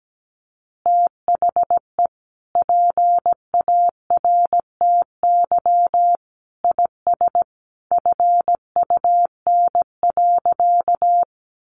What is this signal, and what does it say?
Morse code, heard as a keyed tone, about 17 wpm, 700 Hz. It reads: THE PARTY IS FUN.
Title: Easy Phrases 17wpm